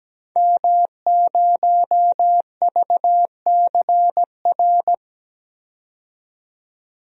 Morse code, heard as a keyed tone, about 17 wpm, 700 Hz. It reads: M0VCR
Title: DXCC-100 Call Signs - 5 Characters - 17wpm